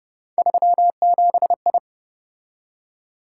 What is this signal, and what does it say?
Morse code, heard as a keyed tone, about 30 wpm, 700 Hz. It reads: 37I